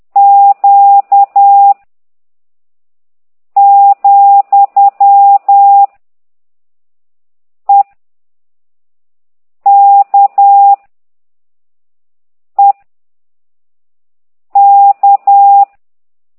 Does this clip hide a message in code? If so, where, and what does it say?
Yes: Morse code from 0.2 to 15.6 s, heard as a keyed tone, about 10 wpm, 800 Hz. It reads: Q , E K E K